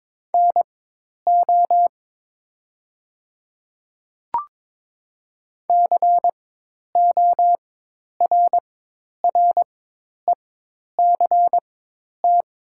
Morse code, heard as a keyed tone, about 22 wpm, spaced as for 10 wpm, 700 Hz. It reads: NO CORRECT
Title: 4x Character Spacing - Top 500 Words - 22wpm